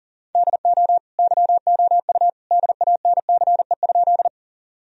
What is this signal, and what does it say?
Morse code, heard as a keyed tone, about 37 wpm, 700 Hz. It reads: DO YOU DANCE?
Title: Sentences from Top 1000 Words - Review Part 4 - 40wpm